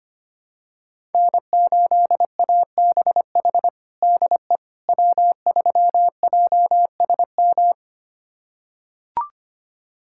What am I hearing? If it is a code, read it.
Morse code, heard as a keyed tone, about 25 wpm, 700 Hz. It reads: N8ABH DE W3JSM
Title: USA Callsign DE Callsign - 5 Characters - 25wpm